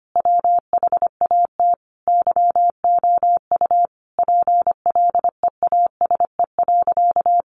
Morse code, heard as a keyed tone, about 25 wpm, 700 Hz. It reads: WHAT YOU PLEASE.
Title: Sentences from Top 1500 Words 25wpm